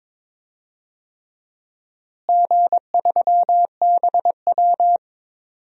Morse code, heard as a keyed tone, about 21 wpm, 700 Hz. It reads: G3BW
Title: DXCC-100 Call Signs - 4 Characters - Repeated 3x - 22wpm